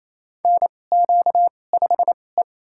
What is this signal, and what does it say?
Morse code, heard as a keyed tone, about 28 wpm, spaced as for 20 wpm, 700 Hz. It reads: NQ5E
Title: USA Call Signs - 2x Character Spacing - 28wpm